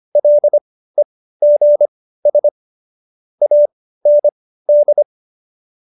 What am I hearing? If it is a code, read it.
Morse code, heard as a keyed tone, about 25 wpm, spaced as for 15 wpm, 590 Hz. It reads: LEGS AND